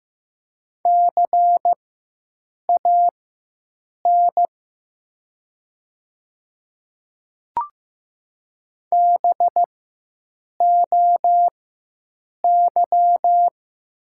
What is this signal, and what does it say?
Morse code, heard as a keyed tone, about 15 wpm, spaced as for 7 wpm, 700 Hz. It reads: CAN BOY
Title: 4x Character Spacing - Top 200 Words - 15wpm